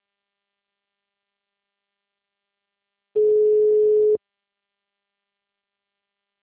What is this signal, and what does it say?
Ringback tone (Europe, 425 Hz).
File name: out- -2030- - - .3047